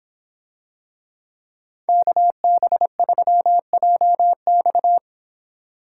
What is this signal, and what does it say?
Morse code, heard as a keyed tone, about 24 wpm, 700 Hz. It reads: KB3JX